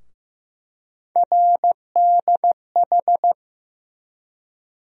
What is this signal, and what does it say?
Morse code, heard as a keyed tone, about 15 wpm, 700 Hz. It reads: RDH
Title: USA Call Sign Postfixes - 3 Letters 15wpm